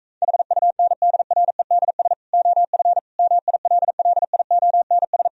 Morse code, heard as a keyed tone, about 38 wpm, 700 Hz. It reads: HUNDREDS OF MILLIONS